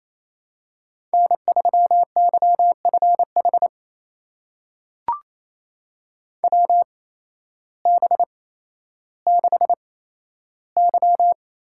Morse code, heard as a keyed tone, about 27 wpm, spaced as for 13 wpm, 700 Hz. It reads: N3YFH WB6Y
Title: USA Call Signs - 8x Character Spacing - 28wpm